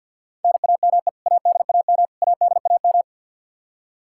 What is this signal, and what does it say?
Morse code, heard as a keyed tone, about 51 wpm, 700 Hz. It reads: NAME ADAM ADAM
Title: K1USN SST QSO Practice 50wpm